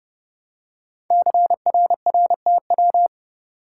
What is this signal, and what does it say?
Morse code, heard as a keyed tone, about 30 wpm, 700 Hz. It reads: CRRTW
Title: Code Groups - 5 Letters - 30wpm